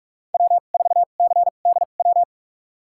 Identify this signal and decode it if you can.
Morse code, heard as a keyed tone, about 44 wpm, spaced as for 32 wpm, 690 Hz. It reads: W4CDW
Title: USA Call Signs - 5 Characters - 2x Character Spacing - 45wpm